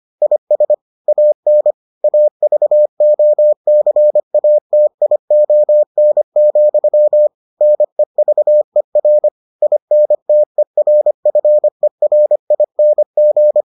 Morse code, heard as a keyed tone, about 25 wpm, 600 Hz. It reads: IS AN AVOCATION, NEVER INTERFERING